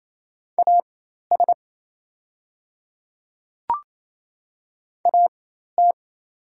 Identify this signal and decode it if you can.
Morse code, heard as a keyed tone, about 28 wpm, spaced as for 13 wpm, 700 Hz. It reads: AS AT
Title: 4x Character Spacing - Top 100 Words - 28wpm